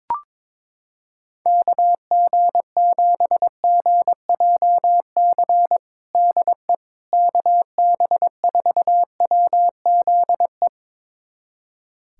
Morse code, heard as a keyed tone, about 22 wpm, 700 Hz. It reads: KG7GJC DE KB4WZE
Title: USA Callsign DE Callsign 22wpm